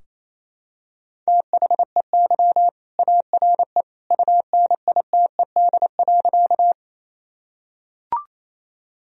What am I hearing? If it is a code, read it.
Morse code, heard as a keyed tone, about 28 wpm, 700 Hz. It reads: THEY ARE UNITED.
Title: Sentences from Top 1000 Words - Review Part 4 - 28wpm